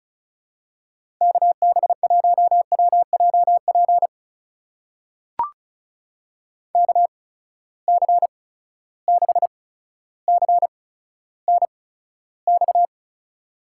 Morse code, heard as a keyed tone, about 36 wpm, spaced as for 15 wpm, 700 Hz. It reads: KB1WJP KC6CNX
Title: USA Call Signs - 6 Characters - 8x Character Spacing - 35wpm